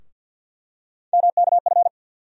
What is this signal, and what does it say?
Morse code, heard as a keyed tone, about 52 wpm, 700 Hz. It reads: MKF
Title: USA Call Sign Postfixes - 3 Letters 50wpm